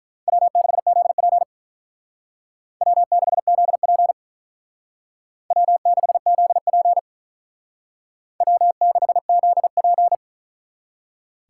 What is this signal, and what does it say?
Morse code, heard as a keyed tone, about 45 wpm, 700 Hz. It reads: W6ZP W6ZP W6ZP W6ZP